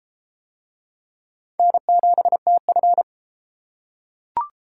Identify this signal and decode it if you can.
Morse code, heard as a keyed tone, about 28 wpm, 700 Hz. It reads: N7TF